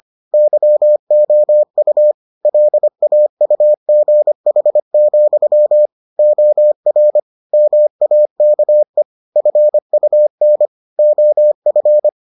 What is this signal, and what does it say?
Morse code, heard as a keyed tone, about 25 wpm, 600 Hz. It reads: YOU LAUGH, OR MAKE FUN OF